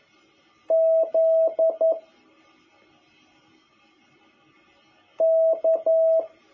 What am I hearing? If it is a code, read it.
Morse code, heard as a keyed tone, about 11 wpm, 650 Hz. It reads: Z K
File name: jh8cbh / 2021cw